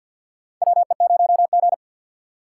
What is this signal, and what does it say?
Morse code, heard as a keyed tone, about 48 wpm, 700 Hz. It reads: WE0G
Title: USA Call Signs - 4 Characters - 3x Character Spacing - 50wpm